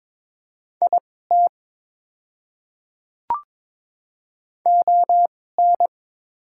Morse code, heard as a keyed tone, about 22 wpm, spaced as for 16 wpm, 700 Hz. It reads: IT ON
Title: 2x Character Spacing - Top 100 Words - 22wpm